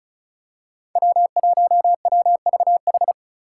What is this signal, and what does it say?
Morse code, heard as a keyed tone, about 34 wpm, 700 Hz. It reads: W1WVH
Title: Five Challenging Letter-Number Combinations 35wpm